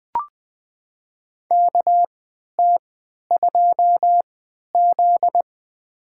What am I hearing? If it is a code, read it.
Morse code, heard as a keyed tone, about 21 wpm, spaced as for 11 wpm, 700 Hz. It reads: KT2Z
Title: USA Call Signs - 3x Character Spacing - 20wpm